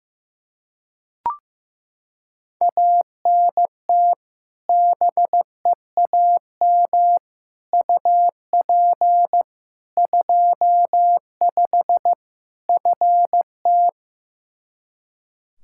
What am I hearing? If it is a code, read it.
Morse code, heard as a keyed tone, about 15 wpm, 700 Hz. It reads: ANT BEAM UP 25 FT
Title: K1USN SST QSO Practice 15wpm